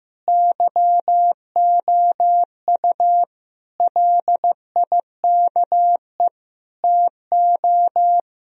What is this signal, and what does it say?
Morse code, heard as a keyed tone, about 16 wpm, 700 Hz. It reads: YOU LIKE TO
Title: Speed-Racing - Sets of 4 Words 15wpm